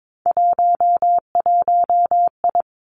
Morse code, heard as a keyed tone, about 22 wpm, 700 Hz. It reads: 11I
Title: Three Challenging Letter-Number Combinations 22wpm